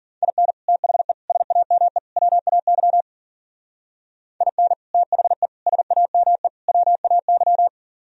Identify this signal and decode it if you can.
Morse code, heard as a keyed tone, about 50 wpm, 700 Hz. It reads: IN THE SAME WAY IN THE SAME WAY
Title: Speed-Racing - Sets of 4 Words 40wpm